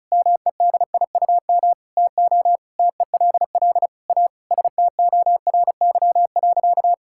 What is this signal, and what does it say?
Morse code, heard as a keyed tone, about 34 wpm, 700 Hz. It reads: MEDIUM TO TELL A STORY.
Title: Sentences from Top 1000 Words - Review Part 6 - 35wpm